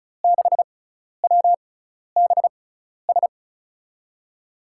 Morse code, heard as a keyed tone, about 36 wpm, spaced as for 12 wpm, 700 Hz. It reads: 6WBS
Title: USA Call Signs - 6 Characters - 6x Character Spacing - 35wpm